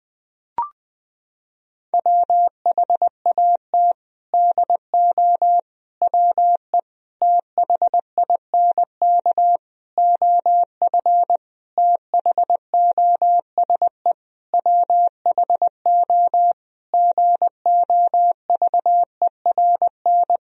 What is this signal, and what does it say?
Morse code, heard as a keyed tone, about 20 wpm, 700 Hz. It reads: WHAT DO WE THINK OF THOSE WHO GOVERN
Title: Sentences from Top 1000 Words - Review Part 2 - 20wpm